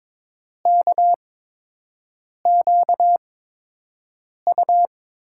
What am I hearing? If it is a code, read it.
Morse code, heard as a keyed tone, about 22 wpm, spaced as for 6 wpm, 700 Hz. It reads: KQU
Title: USA Call Signs - 5 Characters - 8x Character Spacing - 22wpm